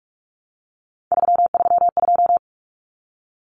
Morse code, heard as a keyed tone, about 44 wpm, 700 Hz. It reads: 332
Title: Common Cut Numbers - 3-Digit 45wpm